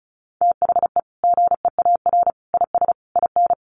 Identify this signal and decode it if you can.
Morse code, heard as a keyed tone, about 34 wpm, 700 Hz. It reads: THE GEAR IS IN
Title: Sentences from Top 1400 Words 35wpm